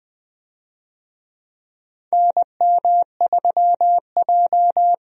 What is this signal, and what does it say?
Morse code, heard as a keyed tone, about 20 wpm, 700 Hz. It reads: NM3J
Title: USA Call Signs - 4 Characters - 20wpm